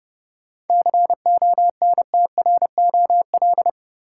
Morse code, heard as a keyed tone, about 32 wpm, 700 Hz. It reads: CONTROL
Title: Sentences from Top 1000 Words - Review Part 2 - New Word Drill - 30wpm